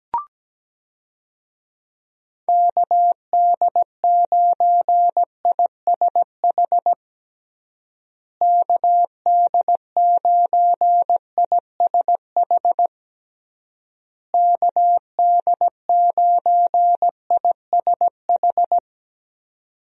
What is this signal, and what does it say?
Morse code, heard as a keyed tone, about 17 wpm, 700 Hz. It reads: KD9ISH KD9ISH KD9ISH
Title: USA Call Signs - 6 Characters - Repeated 3x - 17wpm